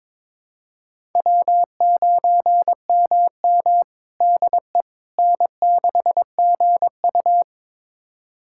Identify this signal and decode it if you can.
Morse code, heard as a keyed tone, about 22 wpm, 700 Hz. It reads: W9MM DE N6GU
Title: USA Callsign DE Callsign - 4 Characters - 22wpm